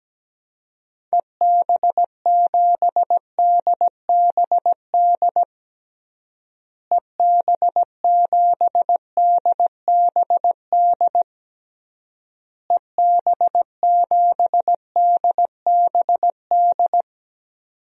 Morse code, heard as a keyed tone, about 17 wpm, 700 Hz. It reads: EB7DBD EB7DBD EB7DBD
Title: DXCC-100 Call Signs - 6 Characters - Repeated 3x - 17wpm